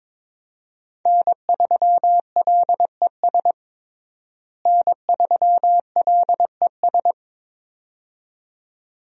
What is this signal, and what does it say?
Morse code, heard as a keyed tone, about 22 wpm, 700 Hz. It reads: N3LES N3LES
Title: USA Call Signs - 5 Characters - Repeated 2x - 22wpm